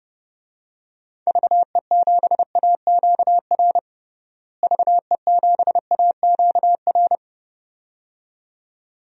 Morse code, heard as a keyed tone, about 30 wpm, 700 Hz. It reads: VE7AQR VE7AQR
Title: DXCC-100 Call Signs - 6 Characters - Repeated 2x - 30wpm